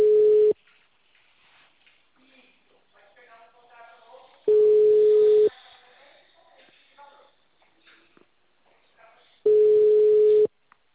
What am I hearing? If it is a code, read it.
Ringback tone (Europe, 425 Hz), with speech.